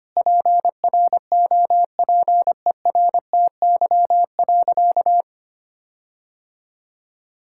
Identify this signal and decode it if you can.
Morse code, heard as a keyed tone, about 25 wpm, 700 Hz. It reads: PROPERTY.
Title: Sentences from Top 1600 Words 25wpm